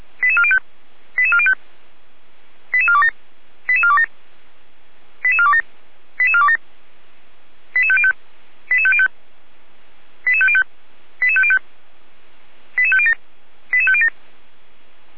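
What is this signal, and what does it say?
Five-tone selective call (ZVEI), heard as quick runs of five tones, tones of about 70 ms. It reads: 88405 88405 88427 88427 88427 88427 88505 88505 88505 88505 88507 88507
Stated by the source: ZVEI five-tone selective call received by SDR, address 88447.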